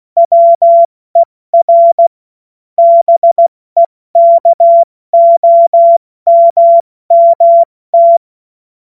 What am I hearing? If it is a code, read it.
Morse code, heard as a keyed tone, about 16 wpm, 680 Hz. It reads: WER BEKOMMT
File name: Witz-21-Oktober-2025-70-F80